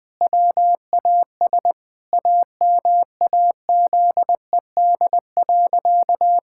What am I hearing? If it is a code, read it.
Morse code, heard as a keyed tone, about 19 wpm, 700 Hz. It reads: WAS AMAZED.
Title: Sentences with 4-Word Phrases - Spoken Beginning - 20wpm